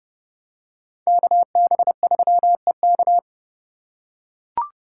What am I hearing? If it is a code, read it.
Morse code, heard as a keyed tone, about 30 wpm, 700 Hz. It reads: KB3EK